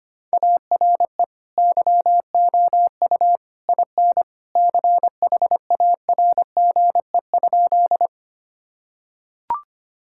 Morse code, heard as a keyed tone, about 25 wpm, 700 Hz. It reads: ARE YOU IN CHARGE?
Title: Sentences from Top 1000 Words - Review Part 6 - 25wpm